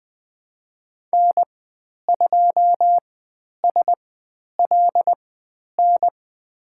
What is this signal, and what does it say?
Morse code, heard as a keyed tone, about 20 wpm, spaced as for 10 wpm, 700 Hz. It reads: N2SLN
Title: 109 Lesson 18g - Letter m - Call Signs 20wpm